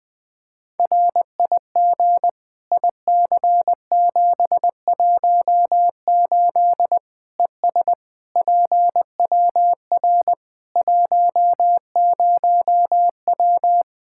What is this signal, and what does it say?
Morse code, heard as a keyed tone, about 20 wpm, 700 Hz. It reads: RIG IC718 ES PWR 10W
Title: Practice QSO - Exchange 2 - Info - 20wpm